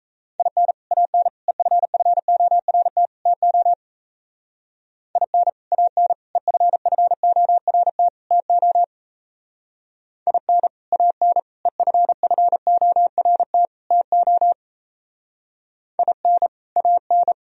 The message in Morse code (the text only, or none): IN AN EFFORT TO IN AN EFFORT TO IN AN EFFORT TO IN AN